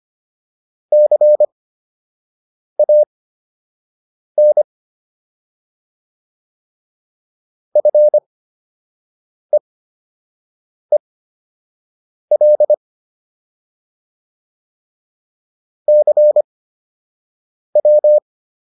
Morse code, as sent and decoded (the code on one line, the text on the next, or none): -.-. .- -. / ..-. . .-.. / -.-. .--
CAN FEEL CW